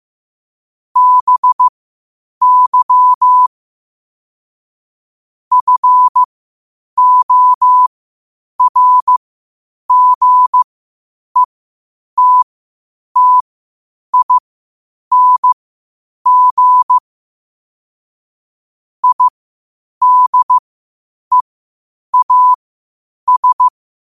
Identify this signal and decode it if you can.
Morse code, heard as a keyed tone, about 15 wpm, spaced as for 8 wpm, 1000 Hz. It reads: BY FORGETTING IDEAS
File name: qotdfiles / 8wpm